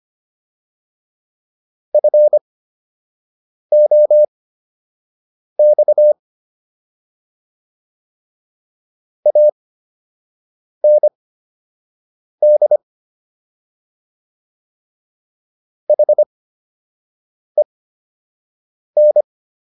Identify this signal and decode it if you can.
Morse code, heard as a keyed tone, about 25 wpm, spaced as for 6 wpm, 600 Hz. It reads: FOX AND HEN